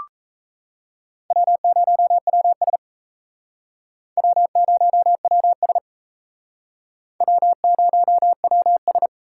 Morse code, heard as a keyed tone, about 34 wpm, 700 Hz. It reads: W0WS W0WS W0WS
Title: Speed Racing - USA 1x2 Call Signs 28wpm